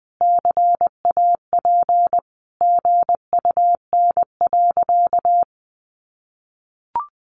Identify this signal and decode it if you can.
Morse code, heard as a keyed tone, about 20 wpm, 700 Hz. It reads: CAP GUN.
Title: A Sound Of Thunder 20wpm